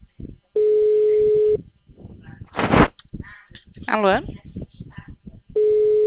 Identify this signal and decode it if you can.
Ringback tone (Europe, 425 Hz), with speech.